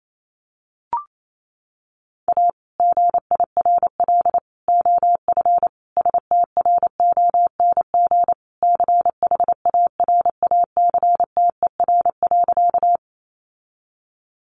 Morse code, heard as a keyed tone, about 28 wpm, 700 Hz. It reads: A GIRL OF STRONG CHARACTER.
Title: Sentences from Top 1200 Words 28wpm